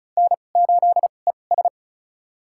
Morse code, heard as a keyed tone, about 34 wpm, spaced as for 25 wpm, 700 Hz. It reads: N8ES
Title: USA Call Signs - 4 Characters - 2x Character Spacing - 35wpm